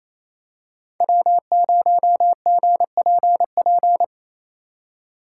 Morse code, heard as a keyed tone, about 27 wpm, 700 Hz. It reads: W0GPP
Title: USA Call Signs - 5 Characters - 6x Character Spacing - 28wpm